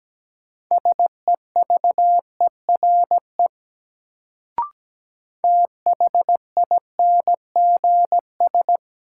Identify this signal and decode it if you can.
Morse code, heard as a keyed tone, about 17 wpm, 700 Hz. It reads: SEVERE THINGS
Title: Sentences from Top 1400 Words - New Word Drill Alt - 17wpm